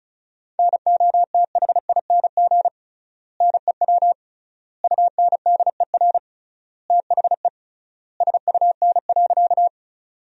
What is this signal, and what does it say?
Morse code, heard as a keyed tone, about 34 wpm, 700 Hz. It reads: NOTHING NEW UNDER THE SUN.
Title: 3x Word-Spacing - Sentences from Top 500 Words - 35wpm